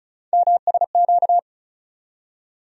Morse code, heard as a keyed tone, about 36 wpm, 700 Hz. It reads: MSQ
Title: USA Call Sign Postfixes - 3 Letters 35wpm